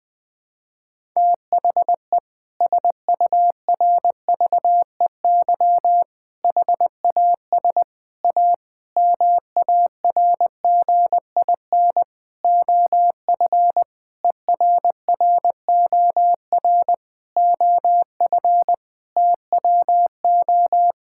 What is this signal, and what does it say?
Morse code, heard as a keyed tone, about 21 wpm, 700 Hz. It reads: THE SURVEY HAS A MARGIN OF ERROR OF TWO
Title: Sentences with 6-Word Phrases - Spoken Beginning - 20wpm